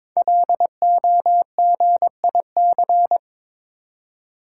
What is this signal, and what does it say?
Morse code, heard as a keyed tone, about 22 wpm, 700 Hz. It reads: LOGIC
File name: 5 Letter Words - First 3 Letters Given - 22wpm